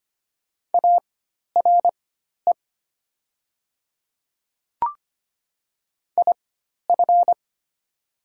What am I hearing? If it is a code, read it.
Morse code, heard as a keyed tone, about 25 wpm, spaced as for 12 wpm, 690 Hz. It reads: ARE IF
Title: 4x Character Spacing - Top 100 Words - 25wpm